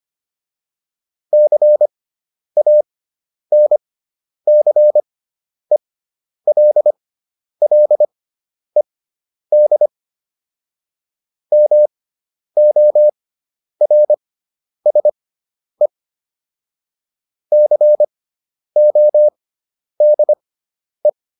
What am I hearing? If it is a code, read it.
Morse code, heard as a keyed tone, about 25 wpm, spaced as for 10 wpm, 600 Hz. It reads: CANCELLED MORSE CODE
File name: SS213 10